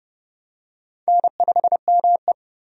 Morse code, heard as a keyed tone, about 30 wpm, 700 Hz. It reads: N5ME
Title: USA Call Signs - 4 Characters - Repeated 3x - 30wpm